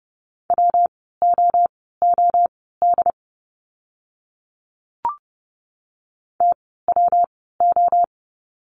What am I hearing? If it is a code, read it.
Morse code, heard as a keyed tone, about 30 wpm, spaced as for 17 wpm, 700 Hz. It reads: WOOD TWO
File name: course / 30:10wpm / 115 Lesson 19f - Letter w - Words 30wpm